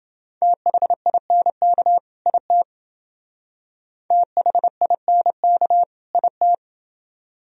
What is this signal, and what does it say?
Morse code, heard as a keyed tone, about 29 wpm, 700 Hz. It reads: THINK IT THINK IT